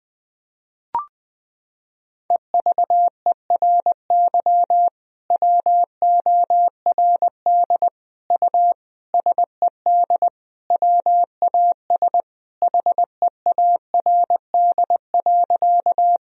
Morse code, heard as a keyed tone, about 20 wpm, 700 Hz. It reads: EVERY WORD U SED WAS HEARD.
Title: Sentences from Top 400 Words and Abbreviations 20wpm